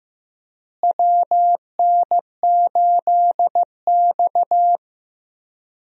Morse code, heard as a keyed tone, about 15 wpm, 700 Hz. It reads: WN8X
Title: USA Call Signs - 4 Characters - Repeated 3x - 15wpm